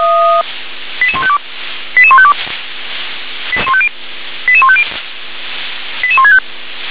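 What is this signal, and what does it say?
Five-tone selective call (ZVEI), heard as quick runs of five tones, tones of about 70 ms. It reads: 88152 88152 88150 88165
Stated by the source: ZVEI five-tone selective call received by SDR.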